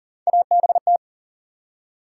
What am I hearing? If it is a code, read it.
Morse code, heard as a keyed tone, about 40 wpm, 700 Hz. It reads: ABT
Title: Top 90 QSO Elements 40wpm